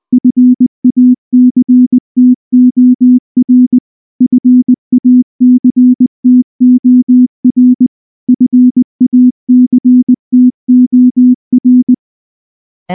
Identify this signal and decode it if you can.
Morse code, heard as a keyed tone, about 19 wpm, 250 Hz. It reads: FACTOR FACTOR FACTOR
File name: morse-training-20WPM